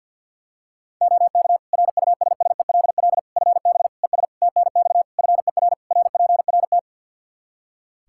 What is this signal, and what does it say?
Morse code, heard as a keyed tone, about 52 wpm, 700 Hz. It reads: OK RUSSELL FB ES TNX FER RPRT
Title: Practice QSO - Exchange 2 - Info - 50wpm